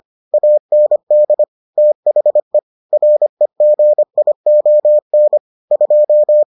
Morse code, heard as a keyed tone, about 25 wpm, 600 Hz. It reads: AND THE REGION 2